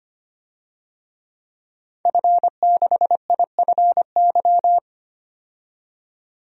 Morse code, heard as a keyed tone, about 25 wpm, 700 Hz. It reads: F6IFY
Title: DXCC-100 Call Signs - 5 Characters - 25wpm